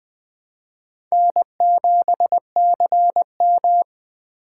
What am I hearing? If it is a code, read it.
Morse code, heard as a keyed tone, about 21 wpm, 700 Hz. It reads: N7CM
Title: USA Call Signs - 4 Characters - 2x Character Spacing - 20wpm